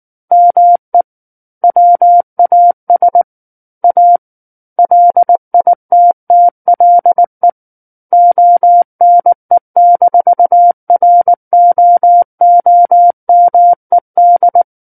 Morse code, heard as a keyed tone, about 19 wpm, 700 Hz. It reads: ME WAS A LITTLE ONE-ROOMED
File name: Book One / 10-Chapter 10 - 19 WPM